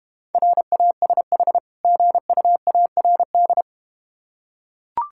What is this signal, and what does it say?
Morse code, heard as a keyed tone, about 31 wpm, 700 Hz. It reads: RASH GUARD
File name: practice / 25wpm / Difficult Words - Speed-Racing with Hints - 25wpm